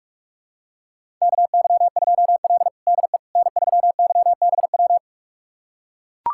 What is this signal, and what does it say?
Morse code, heard as a keyed tone, about 41 wpm, 700 Hz. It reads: KY2L DE N3YBW
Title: USA Callsign DE Callsign 45wpm